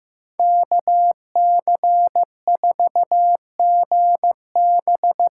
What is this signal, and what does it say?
Morse code, heard as a keyed tone, about 16 wpm, 700 Hz. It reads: KC4GB
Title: USA Call Signs - 6 Characters - 6x Character Spacing - 15wpm